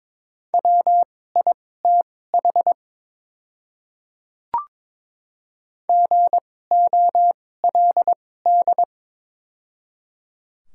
Morse code, heard as a keyed tone, about 23 wpm, spaced as for 16 wpm, 700 Hz. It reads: WITH GOLD
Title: 2x Character Spacing - Top 500 Words - 22wpm